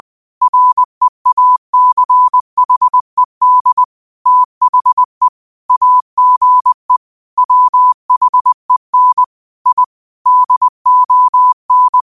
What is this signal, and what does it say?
Morse code, heard as a keyed tone, about 20 wpm, 1000 Hz. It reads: REACHED THE AGE WHEN I DON